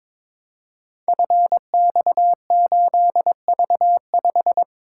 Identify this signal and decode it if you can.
Morse code, heard as a keyed tone, about 22 wpm, 700 Hz. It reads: FX8V5
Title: Code Groups - 5 Letter:Numbers - 22wpm